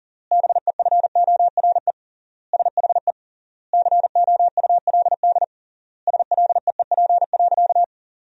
Morse code, heard as a keyed tone, about 40 wpm, 700 Hz. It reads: BEFORE SHE COULD SLEEP.
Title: 3x Word-Spacing - Sentences from Top 500 Words - 40wpm